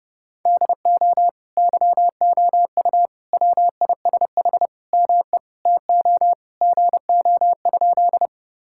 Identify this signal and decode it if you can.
Morse code, heard as a keyed tone, about 30 wpm, 700 Hz. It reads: DO YOU WISH ME TO GO?
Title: Sentences from Top 1600 Words 30wpm